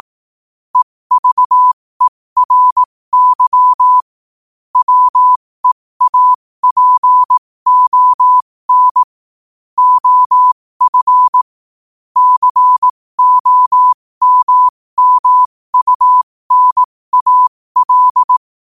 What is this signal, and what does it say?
Morse code, heard as a keyed tone, about 18 wpm, spaced as for 15 wpm, 1000 Hz. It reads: EVERY WEAPON OF COMMUNAL
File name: qotdfiles / 15wpm